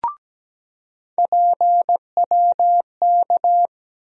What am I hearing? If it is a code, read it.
Morse code, heard as a keyed tone, about 17 wpm, 700 Hz. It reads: PWK
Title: USA Call Sign Postfixes - 3 Letters 17wpm